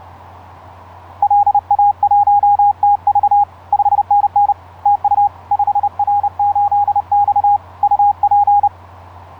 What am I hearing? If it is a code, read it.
Morse code, heard as a keyed tone, about 30 wpm, 820 Hz. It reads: LA1TV 5NN TU 5R8X UP
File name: 5r8x iz5ebl etc 12m cw 07nov2010 1003